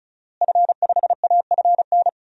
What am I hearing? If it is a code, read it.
Morse code, heard as a keyed tone, about 36 wpm, 700 Hz. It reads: F5AFN
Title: DXCC-100 Call Signs - 5 Characters - Repeated 3x - 35wpm